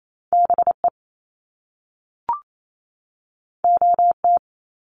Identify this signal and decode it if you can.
Morse code, heard as a keyed tone, about 28 wpm, 700 Hz. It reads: BE OT
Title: Top Two Letter Combinations in Words 28wpm